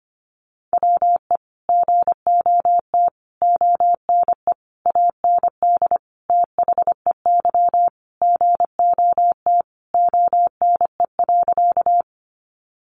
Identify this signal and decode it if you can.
Morse code, heard as a keyed tone, about 25 wpm, 700 Hz. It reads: WE GOT ONE AND THEY GOT ONE.
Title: Sentences from Top 300 Words 25wpm